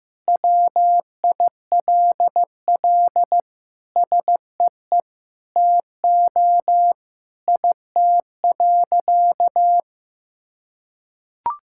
Morse code, heard as a keyed tone, about 15 wpm, 700 Hz. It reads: WILL SEE TO IT.
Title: Sentences from Top 1000 Words - Review Part 5 - 15wpm